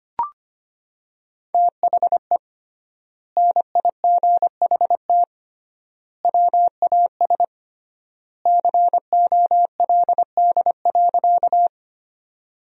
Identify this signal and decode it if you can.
Morse code, heard as a keyed tone, about 25 wpm, 700 Hz. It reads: THE NIGHT WAS COLD.
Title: 3x Word-Spacing - Sentences from Top 400 Words - 25wpm